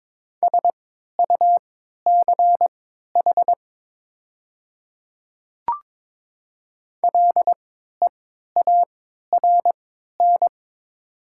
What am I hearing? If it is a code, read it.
Morse code, heard as a keyed tone, about 22 wpm, spaced as for 13 wpm, 700 Hz. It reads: SUCH LEARN
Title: 3x Character Spacing - Top 300 Words - 22wpm